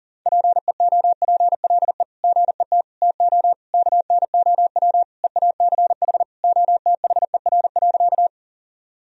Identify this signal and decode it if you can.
Morse code, heard as a keyed tone, about 40 wpm, 700 Hz. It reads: PEOPLE GET TO KNOW EACH OTHER.
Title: Sentences from Top 1000 Words - Review Part 1 - 40wpm